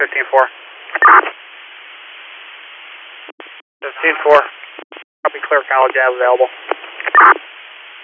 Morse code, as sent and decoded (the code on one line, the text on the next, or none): none